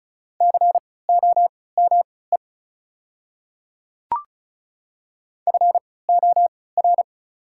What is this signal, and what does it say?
Morse code, heard as a keyed tone, about 36 wpm, spaced as for 20 wpm, 690 Hz. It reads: COME FOR
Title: 3x Character Spacing - Top 100 Words - 35wpm